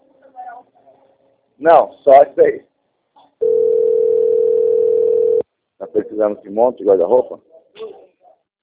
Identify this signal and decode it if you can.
Ringback tone (North America), 440 + 480 Hz, with speech.
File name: out- - - - - .12875